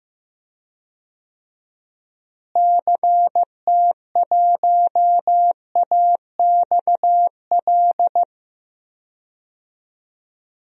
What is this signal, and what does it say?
Morse code, heard as a keyed tone, about 15 wpm, 700 Hz. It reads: CT1AXL
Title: DXCC-100 Call Signs - 6 Characters - Repeated 3x - 15wpm